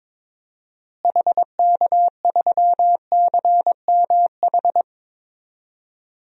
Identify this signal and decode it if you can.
Morse code, heard as a keyed tone, about 22 wpm, 700 Hz. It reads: HK3CMH
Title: DXCC-100 Call Signs - 6 Characters - Repeated 3x - 22wpm